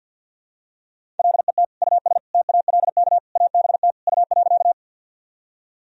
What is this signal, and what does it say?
Morse code, heard as a keyed tone, about 53 wpm, 700 Hz. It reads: LET US TALK ABT U.
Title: Sentences from Top 400 Words and Abbreviations 50wpm